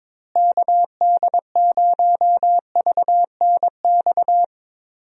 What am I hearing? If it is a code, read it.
Morse code, heard as a keyed tone, about 21 wpm, 700 Hz. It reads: KD0VNX
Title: USA Call Signs - 6 Characters - Repeated 2x - 22wpm